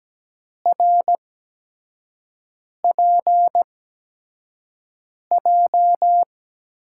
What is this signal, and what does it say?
Morse code, heard as a keyed tone, about 17 wpm, spaced as for 5 wpm, 700 Hz. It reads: RPJ
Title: USA Call Signs - 5 Characters - 8x Character Spacing - 17wpm